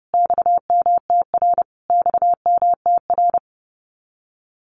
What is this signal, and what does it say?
Morse code, heard as a keyed tone, about 30 wpm, 700 Hz. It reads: XMTR XMTR
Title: Abbreviations Repeated 3x - 30wpm